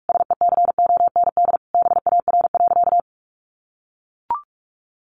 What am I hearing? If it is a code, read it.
Morse code, heard as a keyed tone, about 44 wpm, 700 Hz. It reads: SECOND BAR.
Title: Easy Phrases 45wpm